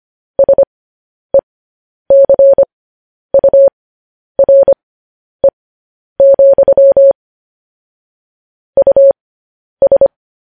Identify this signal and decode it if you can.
Morse code, heard as a keyed tone, about 25 wpm, spaced as for 10 wpm, 550 Hz. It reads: SECURE, US
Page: News Headlines in Morse Code at 10 WPM